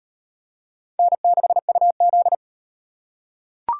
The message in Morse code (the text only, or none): N6UZ